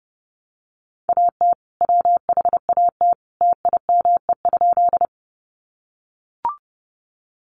Morse code, heard as a keyed tone, about 30 wpm, 700 Hz. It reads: AT WHAT TIME?
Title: Sentences from Top 100 Words 30wpm